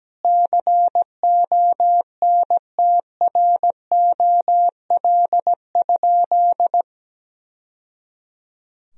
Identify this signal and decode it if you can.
Morse code, heard as a keyed tone, about 17 wpm, 700 Hz. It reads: CONTROL?
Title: Sentences from Top 1000 Words - Review Part 3 - 17wpm